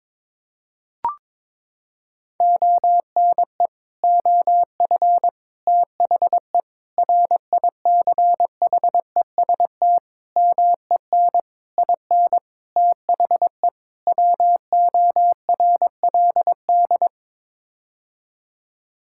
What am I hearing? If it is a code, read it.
Morse code, heard as a keyed tone, about 22 wpm, 700 Hz. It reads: ONE OF THE RICHEST MEN IN THE WORLD
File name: practice / 22wpm / Sets of 8 Words - 22wpm